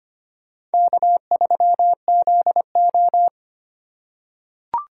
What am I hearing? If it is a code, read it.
Morse code, heard as a keyed tone, about 25 wpm, 700 Hz. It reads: K3ZO
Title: USA Call Signs - 4 Characters - 4x Character Spacing - 25wpm